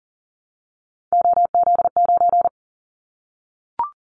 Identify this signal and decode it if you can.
Morse code, heard as a keyed tone, about 40 wpm, 700 Hz. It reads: OZ9
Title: Callsign Prefixes for Top 50 DXCC - 40wpm